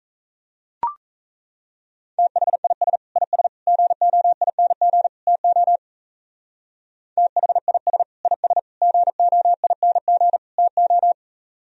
Morse code, heard as a keyed tone, about 35 wpm, 700 Hz. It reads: THIS IS GOING TO THIS IS GOING TO